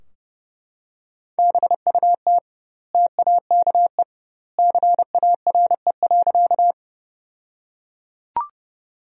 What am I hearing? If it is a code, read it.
Morse code, heard as a keyed tone, about 30 wpm, 700 Hz. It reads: BUT TAKE CARE.
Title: 2x Word-Spacing - Sentences from Top 300 Words - 30wpm